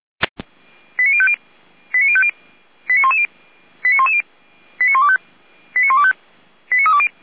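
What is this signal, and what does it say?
Five-tone selective call (ZVEI), heard as quick runs of five tones, tones of about 70 ms. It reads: 89050 89110 89110 89125 89125 89320
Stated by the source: ZVEI five-tone selective call received on an SDR.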